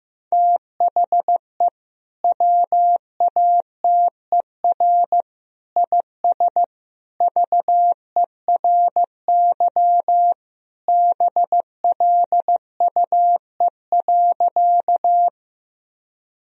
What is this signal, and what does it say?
Morse code, heard as a keyed tone, about 15 wpm, 700 Hz. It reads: THE WATER IS VERY BLUE.
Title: Sentences from Top 1000 Words - Review Part 3 - 15wpm